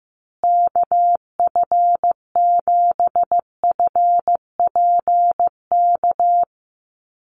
Morse code, heard as a keyed tone, about 15 wpm, 700 Hz. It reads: KF7FPK
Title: USA Call Signs Repeated 2x 15wpm